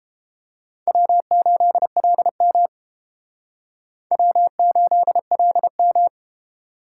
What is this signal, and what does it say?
Morse code, heard as a keyed tone, about 30 wpm, 700 Hz. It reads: W8LM W8LM